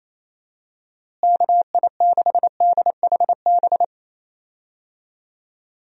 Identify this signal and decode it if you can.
Morse code, heard as a keyed tone, about 28 wpm, 700 Hz. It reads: KI6DHB
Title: USA Call Signs - 6 Characters - Repeated 3x - 28wpm